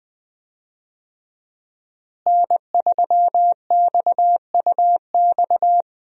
Morse code, heard as a keyed tone, about 21 wpm, 700 Hz. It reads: N3XUX
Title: USA Call Signs - 5 Characters - Repeated 2x - 20wpm